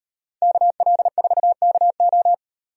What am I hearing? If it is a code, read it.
Morse code, heard as a keyed tone, about 38 wpm, 700 Hz. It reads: KL4KO